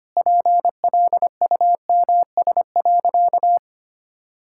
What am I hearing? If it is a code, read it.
Morse code, heard as a keyed tone, about 25 wpm, 700 Hz. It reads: PLUMS.
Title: Pangram Challenge v2 - 25wpm